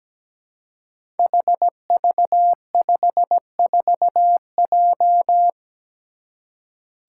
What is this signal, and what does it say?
Morse code, heard as a keyed tone, about 17 wpm, 700 Hz. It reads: HV54J
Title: Five Challenging Letter-Number Combinations 17wpm